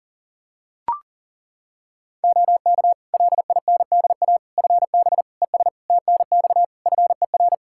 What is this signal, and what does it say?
Morse code, heard as a keyed tone, about 40 wpm, 700 Hz. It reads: OK LINDA FB ES TNX FER
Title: Practice QSO - Exchange 2 - Info - 40wpm